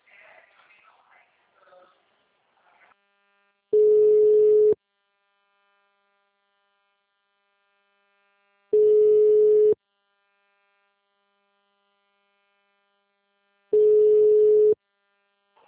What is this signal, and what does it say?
Ringback tone (Europe, 425 Hz).
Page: TLML PRUDENTE 1